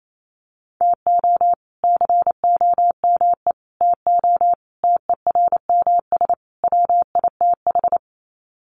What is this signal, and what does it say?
Morse code, heard as a keyed tone, about 28 wpm, 700 Hz. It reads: TO COME TO TERMS WITH
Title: Sets of 5 Words Encore 28wpm